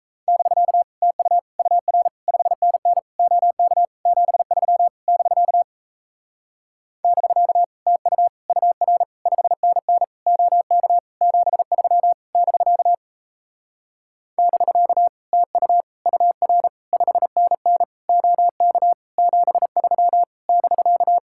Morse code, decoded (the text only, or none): <BK> TU UR 5NN OK 73 <BK> <BK> TU UR 5NN OK 73 <BK> <BK> TU UR 5NN OK 73 <BK>